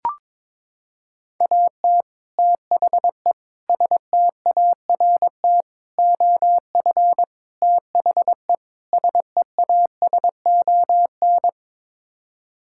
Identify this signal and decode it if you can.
Morse code, heard as a keyed tone, about 22 wpm, 700 Hz. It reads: AT THE START OF THE SEASON
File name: practice / 22wpm / Sets of 6 Words - 22wpm